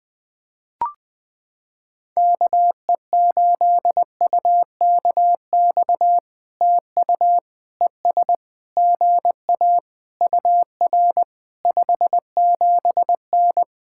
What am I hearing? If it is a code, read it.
Morse code, heard as a keyed tone, about 20 wpm, 700 Hz. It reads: KE8UKX TU ES GA UR 57N
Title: POTA QSO Practice - 20wpm